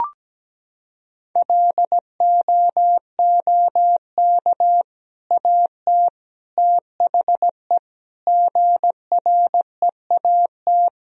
Morse code, heard as a keyed tone, about 17 wpm, 700 Hz. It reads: LOOK AT THE GREAT
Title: Sentences from Top 1000 Words - Review Part 1 - 17wpm